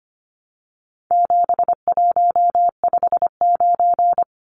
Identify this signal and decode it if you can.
Morse code, heard as a keyed tone, about 25 wpm, 700 Hz. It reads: 7159
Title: Numbers - 4-Digit 25wpm